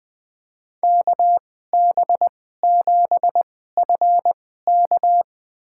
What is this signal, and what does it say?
Morse code, heard as a keyed tone, about 20 wpm, spaced as for 15 wpm, 700 Hz. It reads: KB7FK
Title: USA Call Signs - 5 Characters - 2x Character Spacing - 20wpm